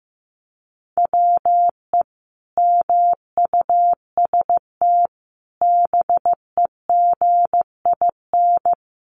Morse code, heard as a keyed tone, about 15 wpm, 700 Hz. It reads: WE MUST BEGIN
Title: Sentences from Top 300 Words 15wpm